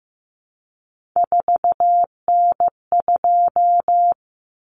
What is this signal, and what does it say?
Morse code, heard as a keyed tone, about 15 wpm, 700 Hz. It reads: 4N2
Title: Common Cut Numbers - 3-Digit 15wpm